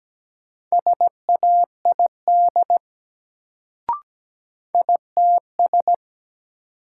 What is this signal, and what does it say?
Morse code, heard as a keyed tone, about 17 wpm, 700 Hz. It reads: SAID ITS
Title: Sets of 2 Words - New Word Drill - 17wpm